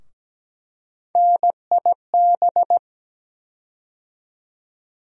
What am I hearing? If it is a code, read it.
Morse code, heard as a keyed tone, about 17 wpm, 700 Hz. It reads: NIB
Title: USA Call Sign Postfixes - 3 Letters 17wpm